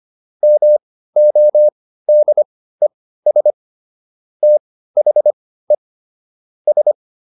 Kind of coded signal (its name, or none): Morse code